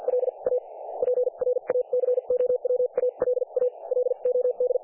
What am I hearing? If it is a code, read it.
Morse code, heard as a keyed tone, about 50 wpm, spaced as for 40 wpm, 500 Hz. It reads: LA PRACOWALA LOD
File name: dlazaawansowanych / nagraniazpasma / websdr recording 3564.7kHz